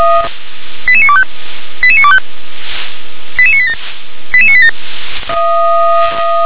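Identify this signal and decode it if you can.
Five-tone selective call (ZVEI), heard as quick runs of five tones, tones of about 70 ms. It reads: 88925 88925 88987 88987
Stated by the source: ZVEI five-tone selective call received by SDR.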